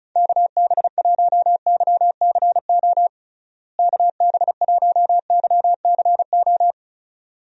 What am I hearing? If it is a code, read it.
Morse code, heard as a keyed tone, about 36 wpm, 700 Hz. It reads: KB1YCO KB1YCO
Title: USA Call Signs - 6 Characters - Repeated 2x - 35wpm